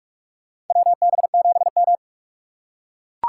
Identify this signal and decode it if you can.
Morse code, heard as a keyed tone, about 44 wpm, 700 Hz. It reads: WB7M
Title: USA Call Signs - 4 Characters - 6x Character Spacing - 45wpm